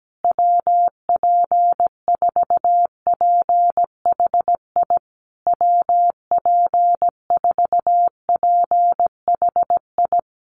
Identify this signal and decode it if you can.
Morse code, heard as a keyed tone, about 17 wpm, 700 Hz. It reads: WP4PHI WP4PHI
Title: USA Call Signs Repeated 3x 17wpm